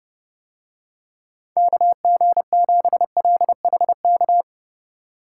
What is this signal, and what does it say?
Morse code, heard as a keyed tone, about 30 wpm, 700 Hz. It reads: KG7LHK